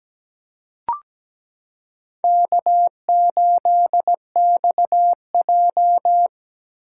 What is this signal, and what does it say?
Morse code, heard as a keyed tone, about 17 wpm, 700 Hz. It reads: K8XJ
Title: K1USN SST QSO Practice - 17wpm